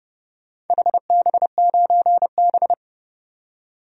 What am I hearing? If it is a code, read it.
Morse code, heard as a keyed tone, about 32 wpm, 700 Hz. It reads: HB9B
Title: DXCC-100 Call Signs - 4 Characters - 30wpm